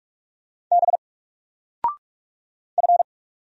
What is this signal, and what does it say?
Morse code, heard as a keyed tone, about 44 wpm, 700 Hz. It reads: B F
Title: Single Character (Letter:Number:Punctuation:Prosign) 45wpm